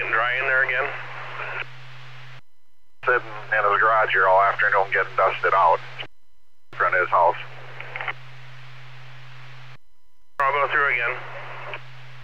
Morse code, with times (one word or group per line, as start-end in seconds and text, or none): none